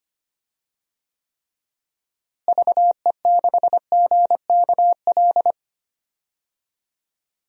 Morse code, heard as a keyed tone, about 25 wpm, 700 Hz. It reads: VE6GKL
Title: DXCC-100 Call Signs - 6 Characters - Repeated 2x - 25wpm